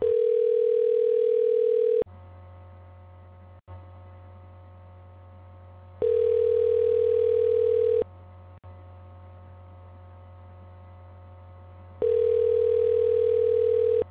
Ringback tone (North America), 440 + 480 Hz.